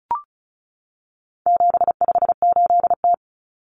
Morse code, heard as a keyed tone, about 34 wpm, 700 Hz. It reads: 758T
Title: Common Cut Numbers - 4-Digit 35wpm